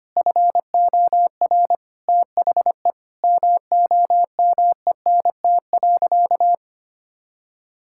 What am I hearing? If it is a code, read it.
Morse code, heard as a keyed tone, about 25 wpm, 700 Hz. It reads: FOR THE MOMENT.
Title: Sentences from Top 1000 Words - Review Part 6 - 25wpm